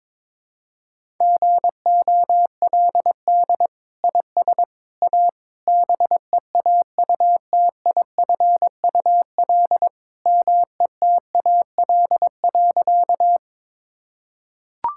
Morse code, heard as a keyed tone, about 22 wpm, 700 Hz. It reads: GOLD IS A BEAUTIFUL METAL.
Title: Sentences from Top 1000 Words - Review Part 6 - 22wpm